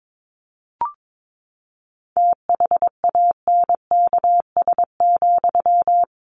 Morse code, heard as a keyed tone, about 22 wpm, 700 Hz. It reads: THANKS,
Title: Sentences from Top 600 Words 22wpm